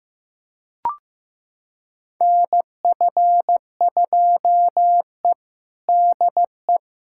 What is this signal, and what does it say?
Morse code, heard as a keyed tone, about 16 wpm, 700 Hz. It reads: NF2E DE
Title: USA Callsign DE Callsign - 4 Characters - 15wpm